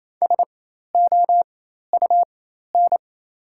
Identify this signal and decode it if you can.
Morse code, heard as a keyed tone, about 28 wpm, spaced as for 13 wpm, 700 Hz. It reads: SOUN
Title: 4x Character Spacing - Top 100 Words - 28wpm